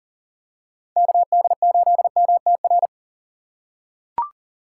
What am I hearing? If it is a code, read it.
Morse code, heard as a keyed tone, about 37 wpm, 700 Hz. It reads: KD8MTR
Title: USA Call Signs - 6 Characters - 3x Character Spacing - 40wpm